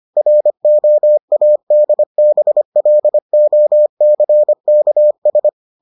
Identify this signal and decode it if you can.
Morse code, heard as a keyed tone, about 25 wpm, 600 Hz. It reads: ROADBLOCKS